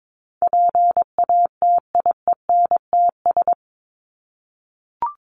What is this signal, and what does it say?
Morse code, heard as a keyed tone, about 22 wpm, 700 Hz. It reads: PATIENTS
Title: ES:S Suffix - Top 100 Words with Base Words 22wpm